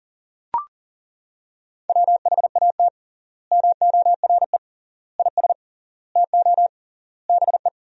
Morse code, heard as a keyed tone, about 40 wpm, 700 Hz. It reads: WHAT MORE IS TO BE
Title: 3x Word-Spacing - Sentences from Top 500 Words - 40wpm